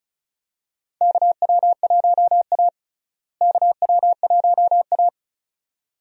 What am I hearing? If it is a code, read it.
Morse code, heard as a keyed tone, about 34 wpm, 700 Hz. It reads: KW1A KW1A
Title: USA Call Signs - 4 Characters - Repeated 2x - 35wpm